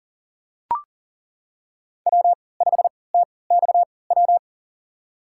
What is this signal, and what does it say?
Morse code, heard as a keyed tone, about 38 wpm, spaced as for 22 wpm, 700 Hz. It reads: W5TXW
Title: USA Call Signs - 5 Characters - 3x Character Spacing - 40wpm